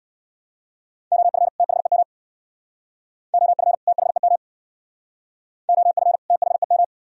Morse code, heard as a keyed tone, about 62 wpm, 700 Hz. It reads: OF THEM OF THEM OF THEM
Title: Speed-Racing - Sets of 2 Words 50wpm